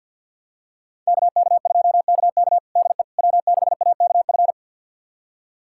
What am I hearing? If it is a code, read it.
Morse code, heard as a keyed tone, about 51 wpm, 700 Hz. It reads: KK2KK DE W6AKF
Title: USA Callsign DE Callsign - 5 Characters - 50wpm